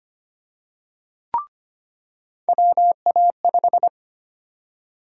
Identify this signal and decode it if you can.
Morse code, heard as a keyed tone, about 25 wpm, 700 Hz. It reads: WA5
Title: USA Call Sign Prefixes - 25wpm